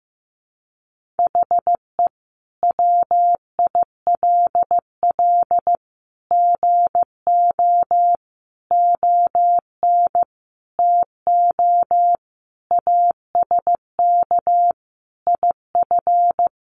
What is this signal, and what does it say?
Morse code, heard as a keyed tone, about 15 wpm, 700 Hz. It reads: HE WILL GO ON TO ASK IF
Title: Sentences from Top 1500 Words 15wpm